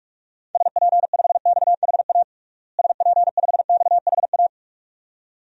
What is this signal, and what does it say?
Morse code, heard as a keyed tone, about 41 wpm, 700 Hz. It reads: SP5XHA SP5XHA
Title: DXCC-100 Call Signs - 6 Characters - Repeated 2x - 45wpm